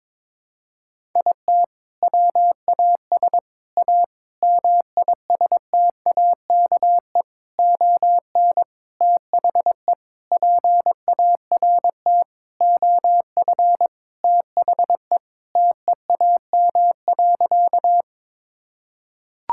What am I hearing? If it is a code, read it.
Morse code, heard as a keyed tone, about 21 wpm, 700 Hz. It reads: IT WAS A MISTAKE ON THE PART OF THE TEAM.
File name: practice / 22wpm / Sentences with 4-Word Phrases - 22wpm